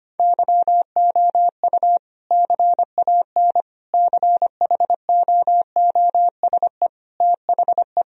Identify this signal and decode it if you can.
Morse code, heard as a keyed tone, about 25 wpm, 700 Hz. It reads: YOU CAN CHOOSE THE
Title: Sentences from Top 1000 Words - Review Part 4 - 25wpm